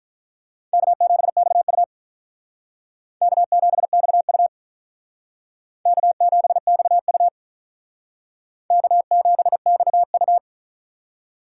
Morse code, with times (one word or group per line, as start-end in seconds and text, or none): 0.7-1.8 K7XU
3.2-4.5 K7XU
5.9-7.3 K7XU
8.7-10.4 K7XU